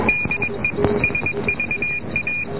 Morse code, heard as a keyed tone, about 21 wpm, 2450 Hz, with noise. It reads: DE SVA